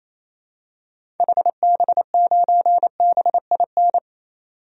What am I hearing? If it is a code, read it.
Morse code, heard as a keyed tone, about 28 wpm, 700 Hz. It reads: HB9BIN
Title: DXCC-100 Call Signs - 6 Characters - 28wpm